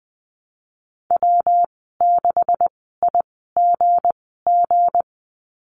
Morse code, heard as a keyed tone, about 20 wpm, spaced as for 15 wpm, 700 Hz. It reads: W6IGG
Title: 173 Lesson 28g - Number 9 - Call Signs 20wpm